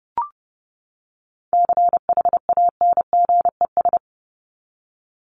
Morse code, heard as a keyed tone, about 30 wpm, 700 Hz. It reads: CHANGES
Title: ES:S Suffix - Top 100 Words with Base Words 30wpm